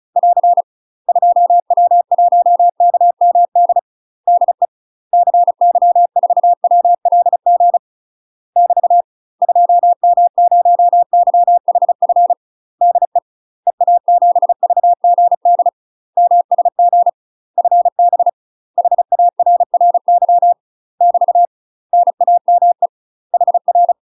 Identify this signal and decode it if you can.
Morse code, heard as a keyed tone, about 36 wpm, 700 Hz. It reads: <AR> 2W1KMD DE CY4WLG <BT> 2M0YHF DE EA7VGD MSG FB HARRY <BT> NAME HR
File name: UK MASTER QSOs TRAINING / 35WPM / UK MASTER QSOs 0011